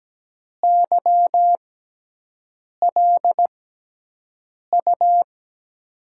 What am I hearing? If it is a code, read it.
Morse code, heard as a keyed tone, about 17 wpm, spaced as for 6 wpm, 700 Hz. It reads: YLU
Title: USA Call Signs - 5 Characters - 6x Character Spacing - 17wpm